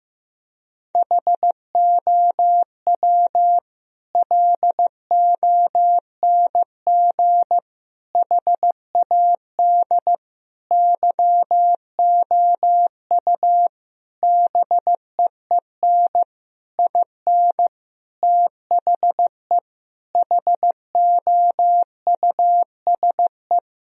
Morse code, heard as a keyed tone, about 15 wpm, 700 Hz. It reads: HOW LONG HAD YOU BEEN IN THE HOUSE
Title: Sentences from Top 1000 Words - Review Part 3 - 15wpm